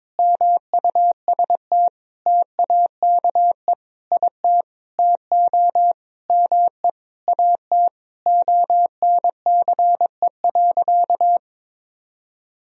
Morse code, heard as a keyed tone, about 22 wpm, 700 Hz. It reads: MUST TAKE IT TO ME AT ONCE.
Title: Sentences from Top 500 Words and Abbreviations 22wpm